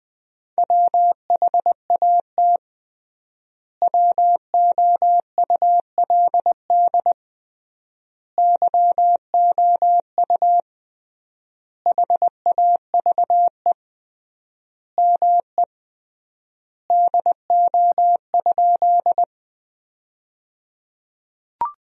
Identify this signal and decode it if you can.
Morse code, heard as a keyed tone, about 20 wpm, 700 Hz. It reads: WHAT WOULD YOU HAVE ME DO?
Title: 3x Word-Spacing - Sentences from Top 200 Words - 20wpm